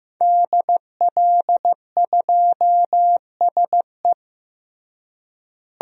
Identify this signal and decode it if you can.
Morse code, heard as a keyed tone, about 15 wpm, 700 Hz. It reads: DL2SE
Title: DXCC-100 Call Signs - 5 Characters - Repeated 2x - 15wpm